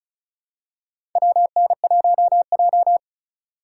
Morse code, heard as a keyed tone, about 36 wpm, 700 Hz. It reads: WN1J
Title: USA Call Signs - 4 Characters - 3x Character Spacing - 35wpm